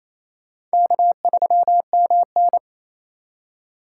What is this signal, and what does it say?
Morse code, heard as a keyed tone, about 35 wpm, 700 Hz. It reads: K3MN
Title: Speed Racing - USA 1x2 Call Signs 28wpm